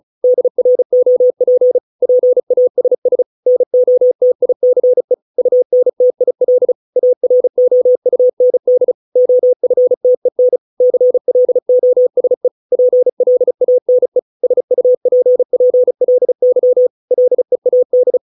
Morse code, heard as a keyed tone, about 34 wpm, 500 Hz. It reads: DROP PASS NOTICE UNTIL AROUND OFTEN CLOSE PLANE SUPPLY LEAD